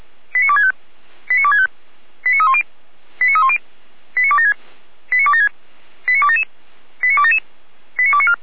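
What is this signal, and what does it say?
Five-tone selective call (ZVEI), heard as quick runs of five tones, tones of about 70 ms. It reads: 89265 89265 89310 89310 89276 89276 89277 89277 89294
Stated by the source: ZVEI five-tone call received by SDR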